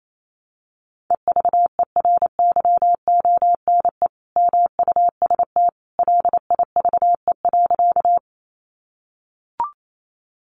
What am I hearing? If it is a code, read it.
Morse code, heard as a keyed tone, about 28 wpm, 700 Hz. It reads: EVERYONE MUST LIVE.
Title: Sentences from Top 800 Words 28wpm